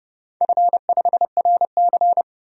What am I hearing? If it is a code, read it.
Morse code, heard as a keyed tone, about 31 wpm, 700 Hz. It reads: F5RC
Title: DXCC-100 Call Signs - 4 Characters - 30wpm